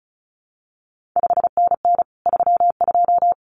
Morse code, heard as a keyed tone, about 36 wpm, 700 Hz. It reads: 5NN 32
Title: CQ WW DX Contest Exchange Encore - 35wpm